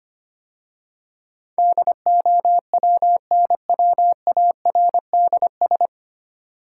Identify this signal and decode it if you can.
Morse code, heard as a keyed tone, about 25 wpm, 700 Hz. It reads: DOWNWARDS
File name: practice / 25wpm / 9 Letter Words - First 3 Letters Given - 25wpm